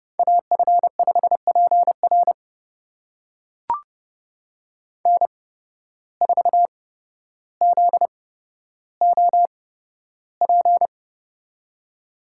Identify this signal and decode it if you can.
Morse code, heard as a keyed tone, about 30 wpm, spaced as for 13 wpm, 700 Hz. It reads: AF5PR N4ZOP
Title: USA Call Signs - 5 Characters - 8x Character Spacing - 30wpm